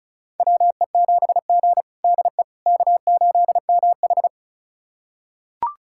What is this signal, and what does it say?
Morse code, heard as a keyed tone, about 36 wpm, 700 Hz. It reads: WE7G DE K8MH
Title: USA Callsign DE Callsign - 4 Characters - 35wpm